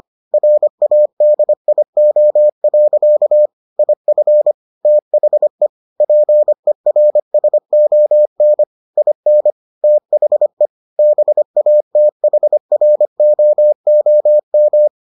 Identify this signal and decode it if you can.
Morse code, heard as a keyed tone, about 25 wpm, 600 Hz. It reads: RADIO. IF THE PERSON IN THE BATHROOM